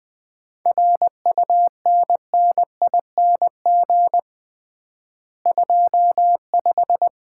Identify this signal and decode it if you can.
Morse code, heard as a keyed tone, about 20 wpm, 700 Hz. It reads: RUNNING 25
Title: Common QSO Phrases - 3x Word-Spacing - 20wpm